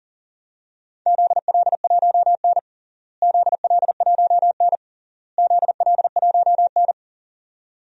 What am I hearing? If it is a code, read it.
Morse code, heard as a keyed tone, about 40 wpm, 700 Hz. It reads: ZL1N ZL1N ZL1N
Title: DXCC-100 Call Signs - 4 Characters - Repeated 3x - 40wpm